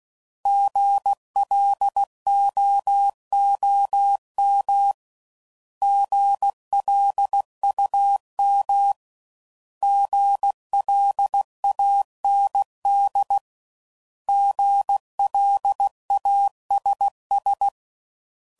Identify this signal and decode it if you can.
Morse code, heard as a keyed tone, about 16 wpm, 800 Hz. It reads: GLOOM GLUM GLAND GLASS
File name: Prefixes / By First Letter / Gs